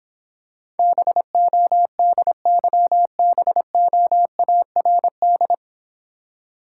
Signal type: Morse code

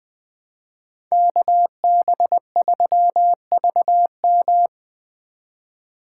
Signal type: Morse code

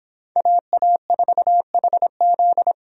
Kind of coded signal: Morse code

